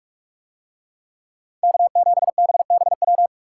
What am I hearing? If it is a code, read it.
Morse code, heard as a keyed tone, about 41 wpm, 700 Hz. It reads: K7BBW